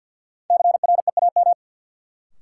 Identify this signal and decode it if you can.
Morse code, heard as a keyed tone, about 53 wpm, 700 Hz. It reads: CREAM
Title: Sentences from Top 1500 Words - New Word Drill - 50wpm